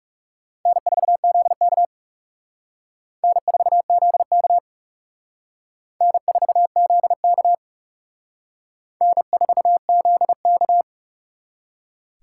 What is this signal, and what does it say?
Morse code, heard as a keyed tone, about 37 wpm, 700 Hz. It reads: N4ZK N4ZK N4ZK N4ZK
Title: Speed Racing - USA 1x2 Call Signs 30wpm